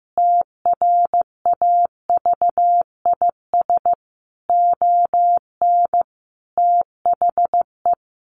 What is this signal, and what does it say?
Morse code, heard as a keyed tone, about 15 wpm, 700 Hz. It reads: TRAVIS ON THE
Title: A Sound Of Thunder 15wpm